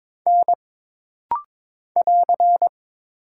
Morse code, heard as a keyed tone, about 22 wpm, 700 Hz. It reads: N <AR>
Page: This rapid-fire timing is geared to push your ICR (Instant Character Recognition) skills.